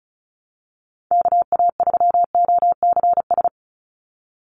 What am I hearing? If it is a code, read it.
Morse code, heard as a keyed tone, about 34 wpm, 700 Hz. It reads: KA3OCS
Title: Call Signs - 35wpm